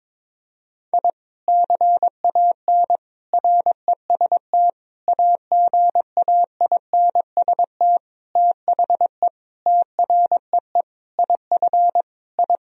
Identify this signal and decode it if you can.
Morse code, heard as a keyed tone, about 22 wpm, 700 Hz. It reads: I CAN REST AGAINST THE TREE IF I
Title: Sentences from Top 1000 Words - Review Part 1 - 22wpm